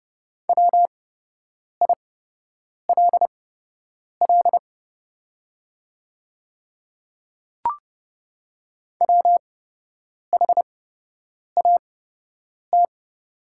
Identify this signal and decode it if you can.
Morse code, heard as a keyed tone, about 30 wpm, spaced as for 8 wpm, 700 Hz. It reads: WILL WHAT
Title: 8x Character Spacing - Top 100 Words - 30wpm